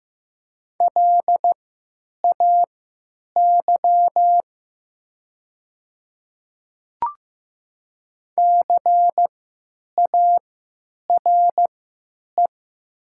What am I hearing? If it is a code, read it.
Morse code, heard as a keyed tone, about 15 wpm, spaced as for 9 wpm, 700 Hz. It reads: LAY CARE
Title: 3x Character Spacing - Top 500 Words - 15wpm